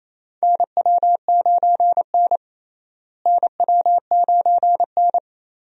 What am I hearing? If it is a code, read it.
Morse code, heard as a keyed tone, about 28 wpm, 700 Hz. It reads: NW9N NW9N
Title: USA Call Signs - 4 Characters - Repeated 3x - 28wpm